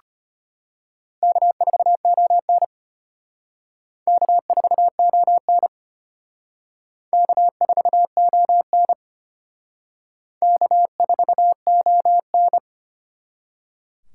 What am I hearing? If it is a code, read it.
Morse code, heard as a keyed tone, about 34 wpm, 700 Hz. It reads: K4ON K4ON K4ON K4ON